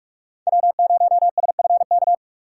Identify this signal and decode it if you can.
Morse code, heard as a keyed tone, about 44 wpm, 700 Hz. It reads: W0SFK
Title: USA Call Signs - 5 Characters - 4x Character Spacing - 45wpm